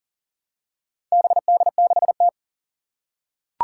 Morse code, heard as a keyed tone, about 37 wpm, 700 Hz. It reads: BD6T
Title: Four Challenging Letter-Number Combinations 40wpm